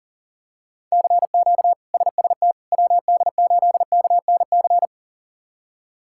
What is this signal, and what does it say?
Morse code, heard as a keyed tone, about 37 wpm, 700 Hz. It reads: CQ SST WD8KNC
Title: K1USN SST QSO Practice - 40wpm